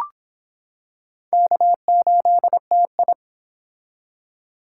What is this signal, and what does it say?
Morse code, heard as a keyed tone, about 25 wpm, 700 Hz. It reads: K8TI